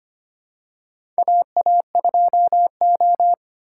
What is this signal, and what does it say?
Morse code, heard as a keyed tone, about 25 wpm, 700 Hz. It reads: AA2O
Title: USA Call Signs - 4x Character Spacing - 25wpm